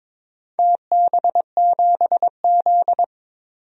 Morse code, heard as a keyed tone, about 23 wpm, 700 Hz. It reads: TB7Z
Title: DXCC-100 Call Signs - 4 Characters - 22wpm